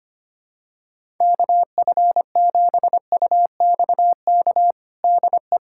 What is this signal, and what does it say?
Morse code, heard as a keyed tone, about 25 wpm, 700 Hz. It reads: KF7UXK DE
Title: USA Callsign DE Callsign 25wpm